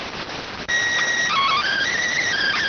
SSTV (slow-scan television), PD 290.